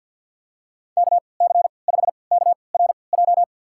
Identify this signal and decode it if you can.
Morse code, heard as a keyed tone, about 52 wpm, spaced as for 29 wpm, 700 Hz. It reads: KC5KRJ
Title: USA Call Signs - 6 Characters - 3x Character Spacing - 50wpm